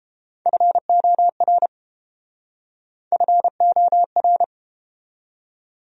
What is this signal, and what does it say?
Morse code, heard as a keyed tone, about 33 wpm, 700 Hz. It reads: FOR FOR